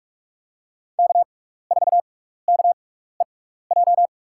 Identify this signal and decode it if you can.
Morse code, heard as a keyed tone, about 45 wpm, spaced as for 16 wpm, 700 Hz. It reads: K4KEJ
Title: USA Call Signs - 5 Characters - 6x Character Spacing - 45wpm